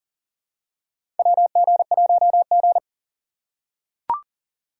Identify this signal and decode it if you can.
Morse code, heard as a keyed tone, about 37 wpm, 700 Hz. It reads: WG1G
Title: USA Call Signs - 4 Characters - 2x Character Spacing - 40wpm